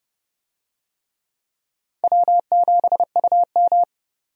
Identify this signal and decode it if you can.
Morse code, heard as a keyed tone, about 30 wpm, 700 Hz. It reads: W7UM
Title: USA Call Signs - 4 Characters - 30wpm